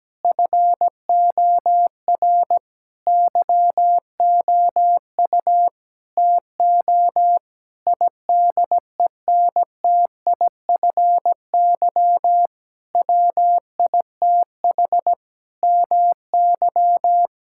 Morse code, heard as a keyed tone, about 17 wpm, 700 Hz. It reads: FOR YOU TO IDENTIFY WITH MY
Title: Sentences from Top 1000 Words - Review Part 6 - 17wpm